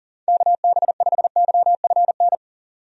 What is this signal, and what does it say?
Morse code, heard as a keyed tone, about 37 wpm, 700 Hz. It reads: KB5YFN
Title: USA Call Signs - 6 Characters - 40wpm